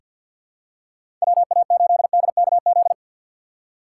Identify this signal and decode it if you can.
Morse code, heard as a keyed tone, about 46 wpm, 700 Hz. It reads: WA8DKZ
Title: USA Call Signs - 6 Characters - 6x Character Spacing - 50wpm